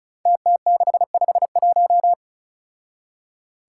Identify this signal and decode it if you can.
Morse code, heard as a keyed tone, about 36 wpm, 700 Hz. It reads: TT651
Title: Five Challenging Letter-Number Combinations 35wpm